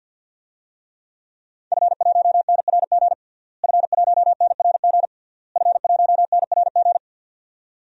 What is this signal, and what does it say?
Morse code, heard as a keyed tone, about 52 wpm, 700 Hz. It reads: F1NRG F1NRG F1NRG
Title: DXCC-100 Call Signs - 5 Characters - Repeated 3x - 50wpm